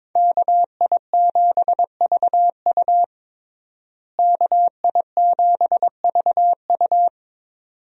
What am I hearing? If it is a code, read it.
Morse code, heard as a keyed tone, about 22 wpm, 700 Hz. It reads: KI7VU KI7VU
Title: USA Call Signs - 5 Characters - Repeated 2x - 22wpm